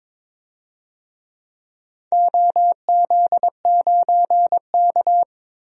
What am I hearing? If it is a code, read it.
Morse code, heard as a keyed tone, about 22 wpm, 700 Hz. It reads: OZ9K
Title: DXCC-100 Call Signs - 4 Characters - Repeated 3x - 22wpm